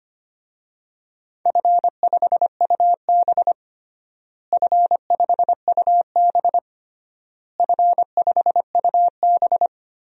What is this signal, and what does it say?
Morse code, heard as a keyed tone, about 25 wpm, 700 Hz. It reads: F5UB F5UB F5UB
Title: DXCC-100 Call Signs - 4 Characters - Repeated 3x - 25wpm